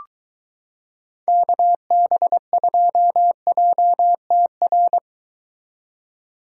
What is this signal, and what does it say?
Morse code, heard as a keyed tone, about 22 wpm, 700 Hz. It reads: KB2JTR